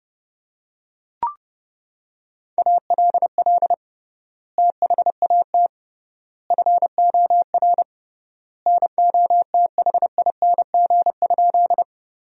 Morse code, heard as a keyed tone, about 32 wpm, 700 Hz. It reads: ALL THAT FOR NOTHING?
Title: 3x Word-Spacing - Sentences from Top 500 Words - 30wpm